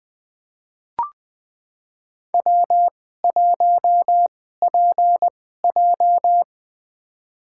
Morse code, heard as a keyed tone, about 20 wpm, spaced as for 14 wpm, 700 Hz. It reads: W1PJ
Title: USA Call Signs - 4 Characters - 2x Character Spacing - 20wpm